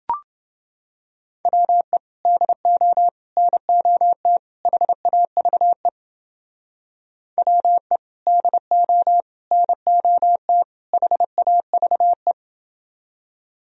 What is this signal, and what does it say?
Morse code, heard as a keyed tone, about 29 wpm, 700 Hz. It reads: WE DO NOT HAVE WE DO NOT HAVE